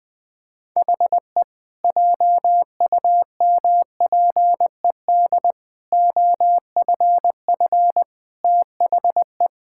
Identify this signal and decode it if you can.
Morse code, heard as a keyed tone, about 20 wpm, 700 Hz. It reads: HE JUMPED OFF THE
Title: Sentences with 2-Word Phrases - 20wpm